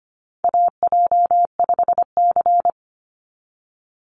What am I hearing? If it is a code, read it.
Morse code, heard as a keyed tone, about 25 wpm, 700 Hz. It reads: AJ5C
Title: NA SOTA Chaser Callsigns 25wpm